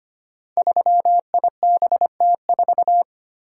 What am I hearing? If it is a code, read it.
Morse code, heard as a keyed tone, about 25 wpm, 700 Hz. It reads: 3IBT4
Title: Code Groups - 5 Letter:Numbers - 25wpm